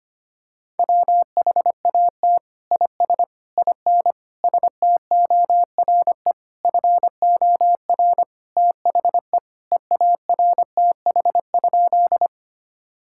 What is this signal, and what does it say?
Morse code, heard as a keyed tone, about 25 wpm, 700 Hz. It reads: WHAT IS IN STORE FOR THE EARTH?
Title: Sentences from Top 1000 Words - Review Part 5 - 25wpm